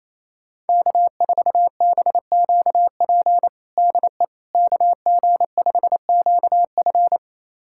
Morse code, heard as a keyed tone, about 27 wpm, 700 Hz. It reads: K4BQP DE KG5QF
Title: USA Callsign DE Callsign 28wpm